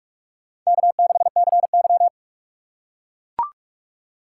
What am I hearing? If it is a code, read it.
Morse code, heard as a keyed tone, about 42 wpm, 700 Hz. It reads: K6CY
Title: USA Call Signs - 4 Characters - 2x Character Spacing - 45wpm